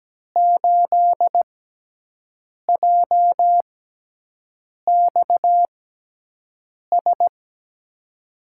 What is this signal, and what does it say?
Morse code, heard as a keyed tone, about 17 wpm, spaced as for 6 wpm, 700 Hz. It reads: 8JXS
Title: USA Call Signs - 6 Characters - 6x Character Spacing - 17wpm